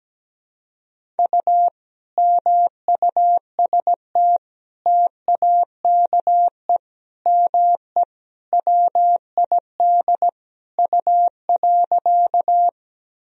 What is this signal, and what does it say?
Morse code, heard as a keyed tone, about 17 wpm, 700 Hz. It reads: U MUST TAKE ME WID U.
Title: Sentences from Top 200 Words and Abbreviations 17wpm